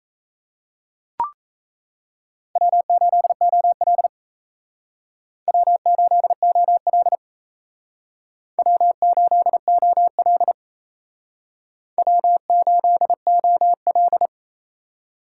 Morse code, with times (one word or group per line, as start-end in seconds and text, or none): 2.6-4.1 W8OL
5.5-7.2 W8OL
8.6-10.5 W8OL
12.0-14.3 W8OL